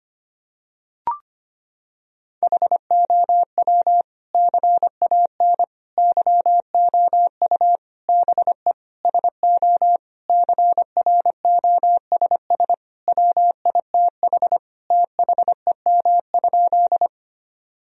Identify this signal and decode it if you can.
Morse code, heard as a keyed tone, about 25 wpm, 700 Hz. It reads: HOW CAN YOU BE SO CROSS WITH THEM?
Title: Sentences from Top 1000 Words - Review Part 1 - 25wpm